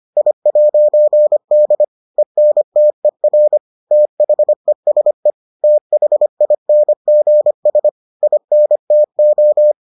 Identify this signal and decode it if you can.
Morse code, heard as a keyed tone, about 25 wpm, 600 Hz. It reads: I'D ENTER THESE THINGS INTO